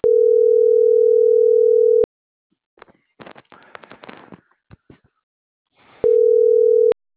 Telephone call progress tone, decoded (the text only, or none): ringback tone